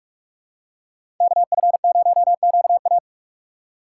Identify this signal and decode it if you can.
Morse code, heard as a keyed tone, about 46 wpm, 690 Hz. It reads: KF0QA
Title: USA Call Signs - 5 Characters - 2x Character Spacing - 45wpm